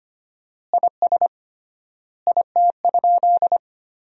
Morse code, heard as a keyed tone, about 25 wpm, 700 Hz. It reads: IS IT?
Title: 3x Word-Spacing - Sentences from Top 100 Words - 25wpm